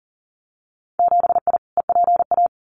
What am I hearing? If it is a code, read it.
Morse code, heard as a keyed tone, about 40 wpm, 700 Hz. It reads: 7I EPA
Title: Winter Field Day Exchanges Encore 40wpm